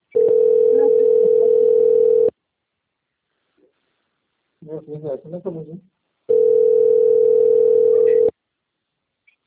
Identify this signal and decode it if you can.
Ringback tone (North America), 440 + 480 Hz, with speech.
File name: out- - - - - .3498